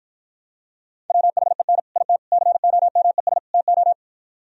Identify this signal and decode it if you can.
Morse code, heard as a keyed tone, about 60 wpm, 700 Hz. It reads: WHEN IT COMES TO